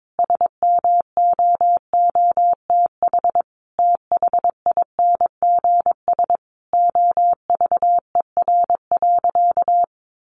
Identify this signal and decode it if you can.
Morse code, heard as a keyed tone, about 22 wpm, 700 Hz. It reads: SMOOTH THINGS OVER.
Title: Sentences from Top 1400 Words 22wpm